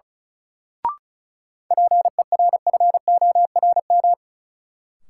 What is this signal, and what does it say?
Morse code, heard as a keyed tone, about 36 wpm, 700 Hz. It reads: PERFORM
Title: Sentences from Top 1000 Words - Review Part 5 - New Word Drill - 35wpm